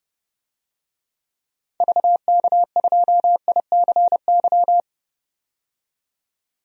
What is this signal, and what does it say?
Morse code, heard as a keyed tone, about 32 wpm, 700 Hz. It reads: VK2ICY
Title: DXCC-100 Call Signs - 6 Characters - 30wpm